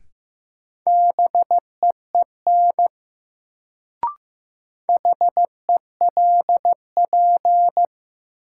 Morse code, heard as a keyed tone, about 15 wpm, 700 Hz. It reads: BEEN HELP
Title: Sets of 2 Words - New Word Drill - 15wpm